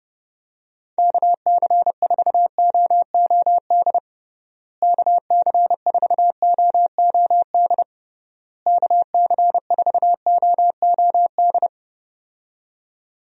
Morse code, heard as a keyed tone, about 29 wpm, 700 Hz. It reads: KC4OOD KC4OOD KC4OOD
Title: USA Call Signs - 6 Characters - Repeated 3x - 30wpm